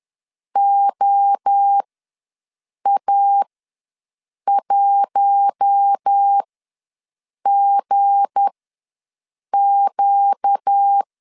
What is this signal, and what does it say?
Morse code, heard as a keyed tone, about 10 wpm, spaced as for 6 wpm, 800 Hz. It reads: OA1GQ
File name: LEZIONI / 7 GF9Z / 7 GF9 ws 10wpm 4034